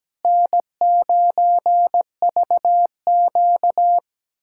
Morse code, heard as a keyed tone, about 17 wpm, 700 Hz. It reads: N9VQ
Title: USA Callsign DE Callsign - 4 Characters - 17wpm